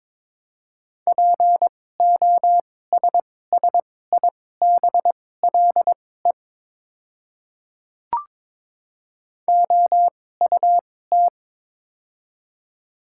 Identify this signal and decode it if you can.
Morse code, heard as a keyed tone, about 22 wpm, spaced as for 16 wpm, 700 Hz. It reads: POSSIBLE OUT
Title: 2x Character Spacing - Top 500 Words - 22wpm